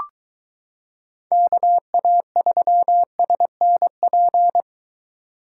Morse code, heard as a keyed tone, about 23 wpm, 700 Hz. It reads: KA3SNP